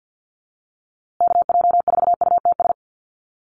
Morse code, heard as a keyed tone, about 50 wpm, 700 Hz. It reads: KJ4UTS
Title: USA Call Signs Repeated 3x 50wpm